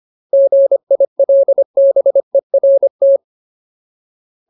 Morse code, heard as a keyed tone, about 25 wpm, 550 Hz. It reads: GILBERT
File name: Ultimate Word List / 25WPM Ultimate